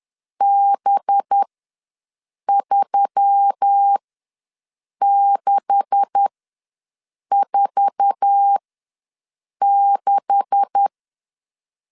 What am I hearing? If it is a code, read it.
Morse code, heard as a keyed tone, about 10 wpm, spaced as for 6 wpm, 800 Hz. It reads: B3646
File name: LEZIONI / 9 286 ws 10wpm 4032